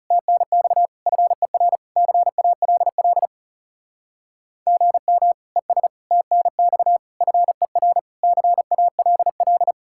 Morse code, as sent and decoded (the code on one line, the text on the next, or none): - -. -..- / ..-. .-. / -.-. .- .-.. .-.. / --. -- / . ... / - -. -..- / ..-. .-. / -.-. .- .-.. .-..
TNX FER CALL GM ES TNX FER CALL